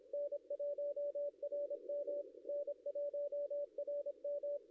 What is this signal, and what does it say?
Morse code, heard as a keyed tone, about 25 wpm, 560 Hz. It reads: N1RM N1RM
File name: contests / ss-cw-k5tr-op-n5rz -0700